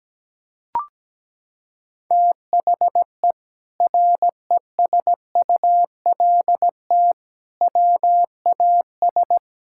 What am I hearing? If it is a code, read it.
Morse code, heard as a keyed tone, about 17 wpm, 700 Hz. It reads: THE RESULT WAS
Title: Sentences from Top 1000 Words - Review Part 5 - 17wpm